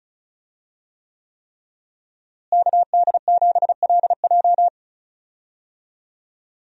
Morse code, heard as a keyed tone, about 36 wpm, 700 Hz. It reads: KD7LJ
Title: USA Call Signs - 5 Characters - Repeated 3x - 35wpm